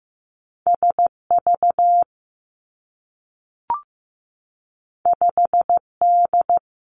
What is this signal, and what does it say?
Morse code, heard as a keyed tone, about 15 wpm, 700 Hz. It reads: S V 5 D
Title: Two Challenging Letter-Number Combinations 15wpm